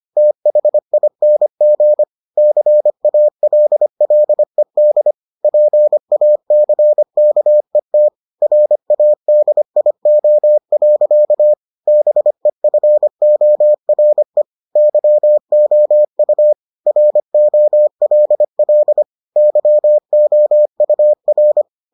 Morse code, heard as a keyed tone, about 25 wpm, 600 Hz. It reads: THING CALLED PACKET RADIO. BEFORE YOU ROLL YOUR